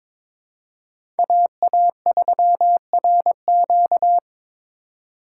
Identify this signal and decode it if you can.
Morse code, heard as a keyed tone, about 22 wpm, 700 Hz. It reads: AA3RQ
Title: USA Call Signs - 5 Characters - 2x Character Spacing - 22wpm